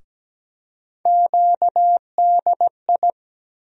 Morse code, heard as a keyed tone, about 17 wpm, 700 Hz. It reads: QDI
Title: USA Call Sign Postfixes - 3 Letters 17wpm